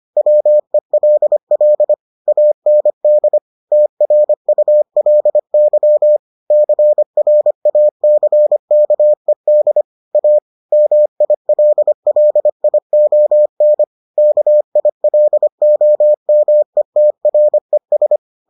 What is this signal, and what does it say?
Morse code, heard as a keyed tone, about 25 wpm, 600 Hz. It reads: WELL AND TRULY CRACKED A MILLION KILOMETRES